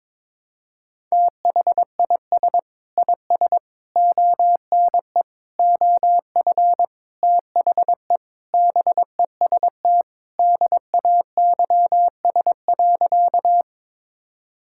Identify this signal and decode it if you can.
Morse code, heard as a keyed tone, about 22 wpm, 700 Hz. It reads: THIS IS ONE OF THE BEST DAYS.
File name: practice / 22wpm / Sentences with 4-Word Phrases - 22wpm